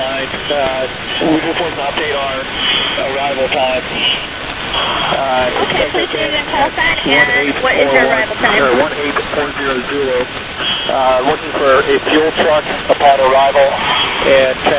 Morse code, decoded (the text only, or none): none